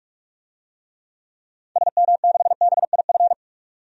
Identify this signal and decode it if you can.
Morse code, heard as a keyed tone, about 45 wpm, 700 Hz. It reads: SM6BIF
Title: DXCC-100 Call Signs - 6 Characters - 45wpm